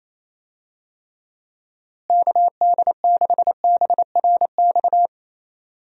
Morse code, heard as a keyed tone, about 28 wpm, 700 Hz. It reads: KD6BRX
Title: USA Call Signs - 6 Characters - Repeated 3x - 28wpm